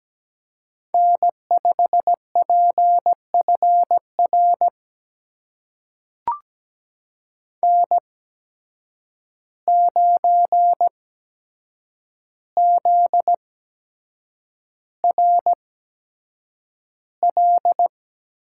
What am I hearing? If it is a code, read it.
Morse code, heard as a keyed tone, about 17 wpm, spaced as for 7 wpm, 700 Hz. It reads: N5PFR N9ZRL
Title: USA Call Signs - 5 Characters - 8x Character Spacing - 17wpm